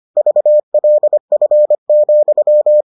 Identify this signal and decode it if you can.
Morse code, heard as a keyed tone, about 25 wpm, 600 Hz. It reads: VLF,